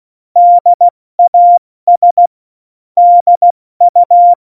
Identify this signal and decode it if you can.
Morse code, heard as a keyed tone, about 16 wpm, 700 Hz. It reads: DAS DU
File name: Morsekurs / Witze / Witz 11. Juni 2019, 70 F80